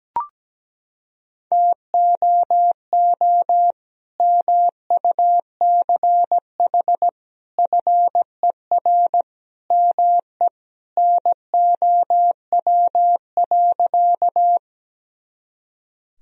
Morse code, heard as a keyed tone, about 17 wpm, 700 Hz. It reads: TOO MUCH FER ME NOW.
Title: Sentences from Top 200 Words and Abbreviations 17wpm